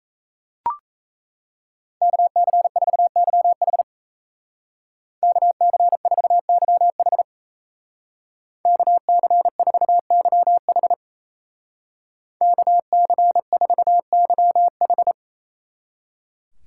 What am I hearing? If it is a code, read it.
Morse code, heard as a keyed tone, about 34 wpm, 700 Hz. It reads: KC4YH KC4YH KC4YH KC4YH
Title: Speed Racing - USA 2x2 Call Signs 28wpm